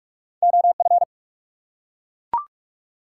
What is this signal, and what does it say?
Morse code, heard as a keyed tone, about 44 wpm, 700 Hz. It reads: OF
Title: Two Letter Words - 45wpm